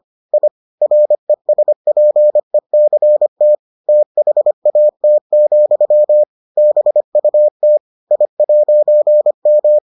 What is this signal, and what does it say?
Morse code, heard as a keyed tone, about 25 wpm, 600 Hz. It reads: I RESPECT THAT, BUT I'M